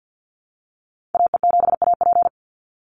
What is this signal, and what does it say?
Morse code, heard as a keyed tone, about 50 wpm, 700 Hz. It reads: AE7AP
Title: NA SOTA Chaser Callsigns 50wpm